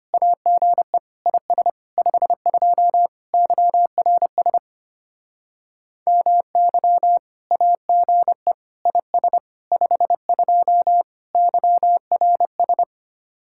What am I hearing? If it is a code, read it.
Morse code, heard as a keyed tone, about 32 wpm, 700 Hz. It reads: AGE IS 52 YRS MY AGE IS 52 YRS